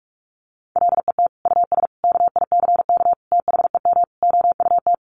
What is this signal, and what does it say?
Morse code, heard as a keyed tone, about 44 wpm, 700 Hz. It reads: LET US KICK THEM OUT
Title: Sentences from Top 1300 Words 45wpm